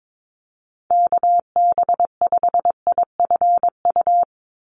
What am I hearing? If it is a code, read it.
Morse code, heard as a keyed tone, about 22 wpm, 700 Hz. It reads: KB5IFU
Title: USA Call Signs 22wpm